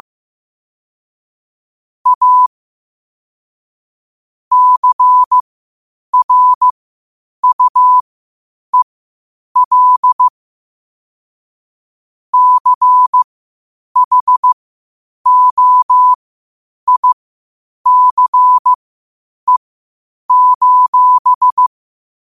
Morse code, heard as a keyed tone, about 15 wpm, spaced as for 8 wpm, 1000 Hz. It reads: A CRUEL CHOICE:
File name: qotdfiles / 8wpm